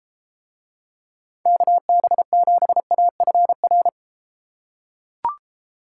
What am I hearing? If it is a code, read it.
Morse code, heard as a keyed tone, about 34 wpm, 700 Hz. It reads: KB7AFR